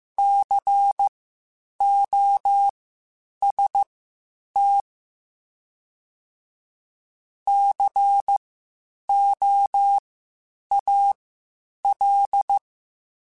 Morse code, heard as a keyed tone, about 15 wpm, spaced as for 8 wpm, 800 Hz. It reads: COST COAL
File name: Spaced Alphas / Spaced Alphas mp3s / Cs sp @13